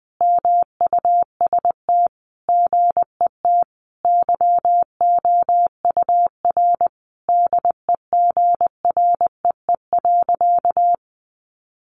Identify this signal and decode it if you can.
Morse code, heard as a keyed tone, about 20 wpm, 700 Hz. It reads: MUST GET YOUR DEGREE.
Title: Sentences from Top 1500 Words 20wpm